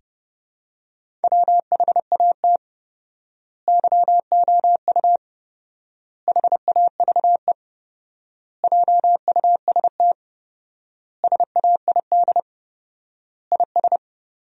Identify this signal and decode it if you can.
Morse code, heard as a keyed tone, about 30 wpm, 700 Hz. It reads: WHAT YOU HAVE JUST SAID IS
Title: 4x Word-Spacing - Sentences from Top 200 Words - 30wpm